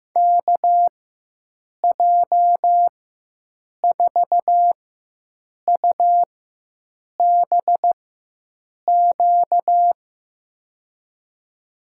Morse code, heard as a keyed tone, about 15 wpm, spaced as for 7 wpm, 700 Hz. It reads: KJ4UBQ
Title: USA Call Signs - 6 Characters - 4x Character Spacing - 15wpm